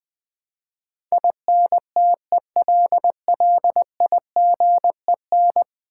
Morse code, heard as a keyed tone, about 21 wpm, 700 Hz. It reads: INTELLIGEN
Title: Build A Word - 5 Letters At A Time - 12-Letter Words - 20wpm